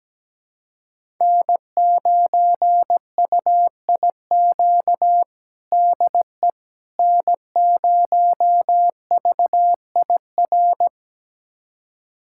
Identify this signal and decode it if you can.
Morse code, heard as a keyed tone, about 17 wpm, 700 Hz. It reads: N9UIQ DE N0VIR
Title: USA Callsign DE Callsign - 5 Characters - 17wpm